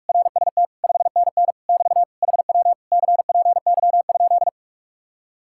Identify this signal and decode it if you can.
Morse code, heard as a keyed tone, about 49 wpm, 700 Hz. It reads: RST 5NN <BT> HW CPY?